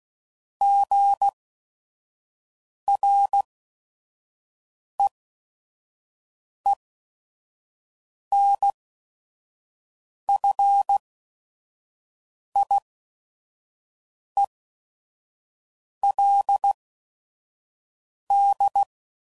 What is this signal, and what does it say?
Morse code, heard as a keyed tone, about 16 wpm, spaced as for 5 wpm, 800 Hz. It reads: GREENFIELD
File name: CW Code Groups @5WPM / fromSM's / Cities